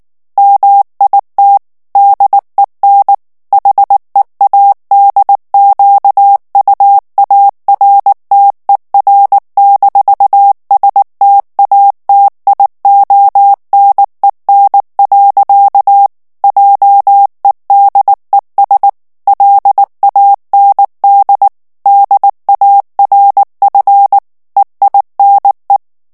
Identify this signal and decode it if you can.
Morse code, heard as a keyed tone, about 19 wpm, 800 Hz. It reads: MIT DEN HEADQUARTER-STATIONEN. JEDES LAND DARF EINE
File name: morse mp3 / ALLEMAND texte clair 2 vitesse 18 mots minute